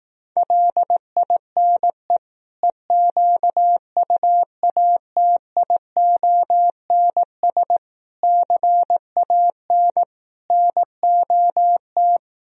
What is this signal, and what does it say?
Morse code, heard as a keyed tone, about 18 wpm, 700 Hz. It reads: LINE EQUATIONS CAN NOT
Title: Speed-Racing - Sets of 5 Words 15wpm